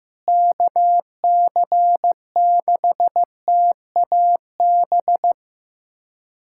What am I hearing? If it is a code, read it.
Morse code, heard as a keyed tone, about 16 wpm, 700 Hz. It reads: KC6TAB
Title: USA Callsign DE Callsign 15wpm